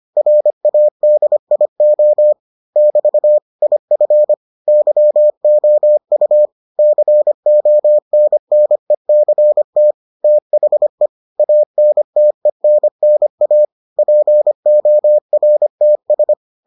Morse code, heard as a keyed tone, about 25 wpm, 600 Hz. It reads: RADIO <BT> IF YOU CONNECT THE ANTENNA PORTS